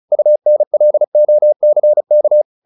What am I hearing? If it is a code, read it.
Morse code, heard as a keyed tone, about 34 wpm, 600 Hz. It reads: UNLOCK